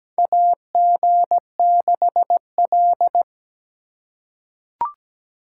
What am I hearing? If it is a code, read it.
Morse code, heard as a keyed tone, about 17 wpm, 700 Hz. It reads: AG6L
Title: USA Call Signs - 3x Character Spacing - 17wpm